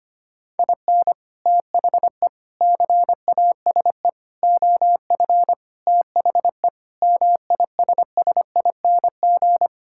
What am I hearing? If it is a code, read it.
Morse code, heard as a keyed tone, about 25 wpm, 700 Hz. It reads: IN THE CASE OF THE MISSING
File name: practice / 25wpm / Sentences with 4-Word Phrases - 25wpm